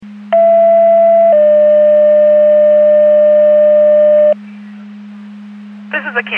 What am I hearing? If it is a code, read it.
Two-tone page, heard as a one-second tone, then a three-second tone, about 670 then 600 Hz. It is heard from 0.3 to 4.3 s.